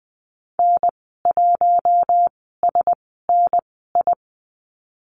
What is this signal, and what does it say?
Morse code, heard as a keyed tone, about 20 wpm, spaced as for 14 wpm, 700 Hz. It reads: N1SNI
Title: 048 Lesson 9g - Number 4 - Call Signs 20wpm